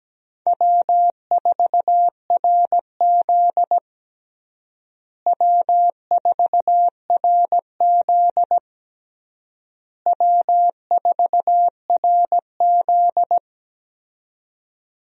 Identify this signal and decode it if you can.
Morse code, heard as a keyed tone, about 17 wpm, 700 Hz. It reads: W4RZ W4RZ W4RZ
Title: USA Call Signs - 4 Characters - Repeated 3x - 17wpm